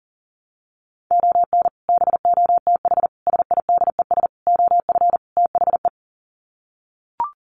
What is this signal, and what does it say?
Morse code, heard as a keyed tone, about 40 wpm, 700 Hz. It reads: ON BOTH SIDES OF THE
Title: Sets of 5 Words 40wpm